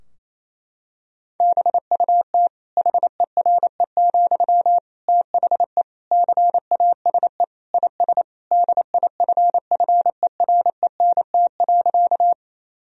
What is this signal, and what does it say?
Morse code, heard as a keyed tone, about 27 wpm, 700 Hz. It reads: BUT HERE, THE CASE IS DIFFERENT.
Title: Sentences from Top 1000 Words - Review Part 3 - 28wpm